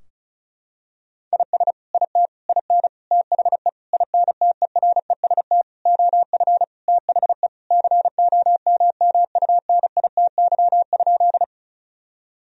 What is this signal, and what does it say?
Morse code, heard as a keyed tone, about 36 wpm, 700 Hz. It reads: IS IT IN THE INTEREST OF THE COMMUNITY?
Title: Sentences from Top 1000 Words - Review Part 4 - 35wpm